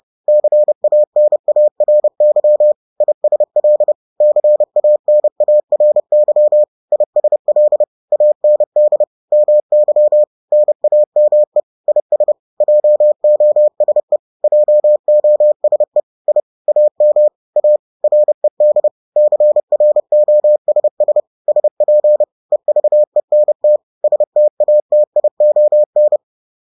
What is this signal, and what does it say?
Morse code, heard as a keyed tone, about 30 wpm, 600 Hz. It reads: CANARY ISL CANARY ISL AND MY NAME IS JOSE JOSE I AM A RED CROSS SP EVENT STATION
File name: 30wpm-7